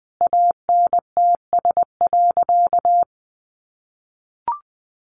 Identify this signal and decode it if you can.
Morse code, heard as a keyed tone, about 20 wpm, 700 Hz. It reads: ANTS.
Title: Additional Easy Phrases 20wpm